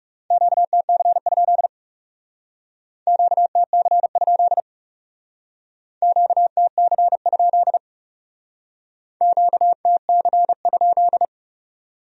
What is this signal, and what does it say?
Morse code, heard as a keyed tone, about 36 wpm, 700 Hz. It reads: QTC? QTC? QTC? QTC?
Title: Speed-Racing Q-Codes - 30wpm